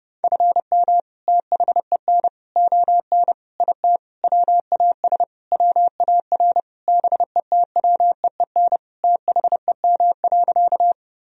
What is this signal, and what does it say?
Morse code, heard as a keyed tone, about 30 wpm, 700 Hz. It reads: FM THEN ON IT WAS WAR BETWEEN THEM.
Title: Sentences from Top 400 Words and Abbreviations 30wpm